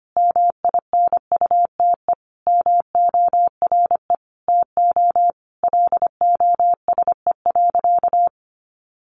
Morse code, heard as a keyed tone, about 25 wpm, 700 Hz. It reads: MINUTE MORE TO LOSE.
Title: Sentences from Top 1100 Words 25wpm